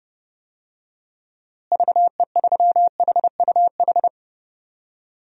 Morse code, heard as a keyed tone, about 30 wpm, 700 Hz. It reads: VE3HUH
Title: DXCC-100 Call Signs - 6 Characters - Repeated 3x - 30wpm